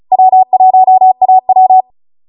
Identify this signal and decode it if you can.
Morse code, heard as a keyed tone, about 36 wpm, 750 Hz. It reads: W1AW